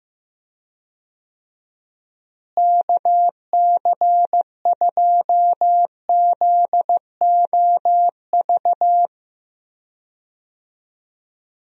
Morse code, heard as a keyed tone, about 15 wpm, 700 Hz. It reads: KC2ZOV
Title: USA Call Signs - 6 Characters - Repeated 2x - 15wpm